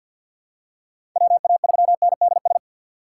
Morse code, heard as a keyed tone, about 48 wpm, 700 Hz. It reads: WA3NDS
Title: USA Call Signs - 6 Characters - 6x Character Spacing - 50wpm